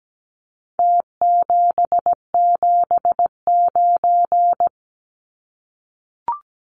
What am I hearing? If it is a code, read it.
Morse code, heard as a keyed tone, about 17 wpm, 700 Hz. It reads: T779
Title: Numbers - Contest Style - 4-Digit 17wpm